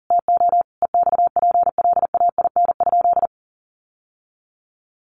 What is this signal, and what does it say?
Morse code, heard as a keyed tone, about 40 wpm, 700 Hz. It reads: TO EXPLAIN?
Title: Sentences from Top 1100 Words 40wpm